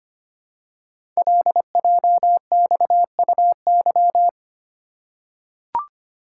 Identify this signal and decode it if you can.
Morse code, heard as a keyed tone, about 25 wpm, 700 Hz. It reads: LJXUY
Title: Code Groups - 5 Letter:Numbers - 25wpm